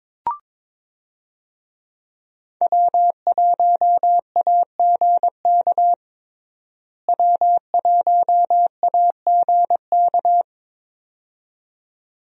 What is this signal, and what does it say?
Morse code, heard as a keyed tone, about 22 wpm, 700 Hz. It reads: W1AGK W1AGK
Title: USA Call Signs - 5 Characters - Repeated 2x - 22wpm